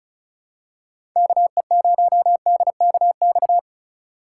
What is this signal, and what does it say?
Morse code, heard as a keyed tone, about 34 wpm, 700 Hz. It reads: KE0DKX
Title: USA Call Signs - 4x Character Spacing - 35wpm